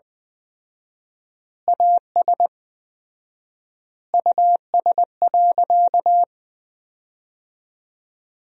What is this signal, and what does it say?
Morse code, heard as a keyed tone, about 20 wpm, 700 Hz. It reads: AS US.
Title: 4x Word-Spacing - Sentences from Top 500 Words - 20wpm